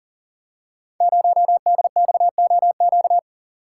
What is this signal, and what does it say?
Morse code, heard as a keyed tone, about 37 wpm, 700 Hz. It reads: 0DXOQ
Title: Code Groups - 5 Letter:Numbers - 40wpm